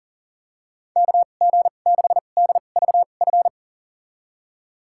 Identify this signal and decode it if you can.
Morse code, heard as a keyed tone, about 40 wpm, spaced as for 29 wpm, 700 Hz. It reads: KG6DVF
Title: USA Call Signs - 6 Characters - 2x Character Spacing - 40wpm